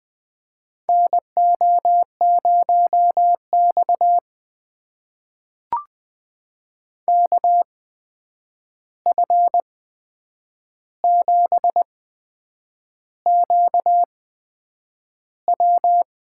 Morse code, heard as a keyed tone, about 20 wpm, spaced as for 8 wpm, 700 Hz. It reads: NO0X KF7QW
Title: USA Call Signs - 8x Character Spacing - 20wpm